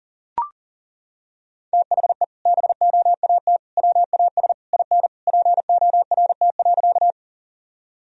Morse code, heard as a keyed tone, about 40 wpm, 700 Hz. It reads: THE BOAT WAS IN PORT.
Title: Sentences from Top 1000 Words - Review Part 1 - 40wpm